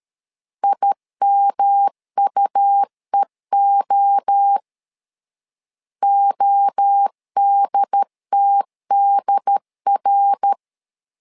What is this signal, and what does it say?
Morse code, heard as a keyed tone, about 13 wpm, 800 Hz. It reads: IMUEO ODTDR
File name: LEZIONI / 3 MWRO / 3 MWRO 12wpm 1730